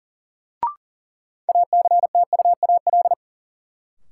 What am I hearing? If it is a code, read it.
Morse code, heard as a keyed tone, about 37 wpm, 700 Hz. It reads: ACTUAL
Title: Sentences from Top 1000 Words - Review Part 5 - New Word Drill - 40wpm